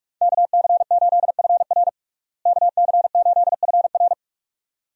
Morse code, heard as a keyed tone, about 41 wpm, 700 Hz. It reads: KC8FR KC8FR
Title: USA Call Signs - 5 Characters - Repeated 2x - 45wpm